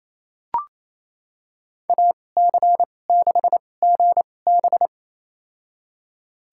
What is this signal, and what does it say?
Morse code, heard as a keyed tone, about 28 wpm, spaced as for 20 wpm, 700 Hz. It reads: AC6GB
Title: USA Call Signs - 5 Characters - 2x Character Spacing - 28wpm